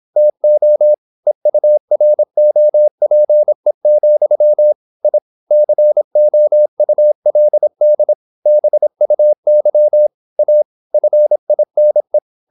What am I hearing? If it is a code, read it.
Morse code, heard as a keyed tone, about 26 wpm, 600 Hz. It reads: TO EUROPE, I COULD BUY A FINE
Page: Never 26wpm | CW med Gnister